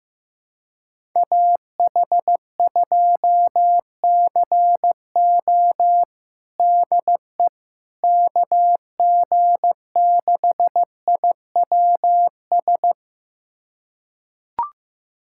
Morse code, heard as a keyed tone, about 15 wpm, 700 Hz. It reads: AH2CO DE KG6IWS
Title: USA Callsign DE Callsign 15wpm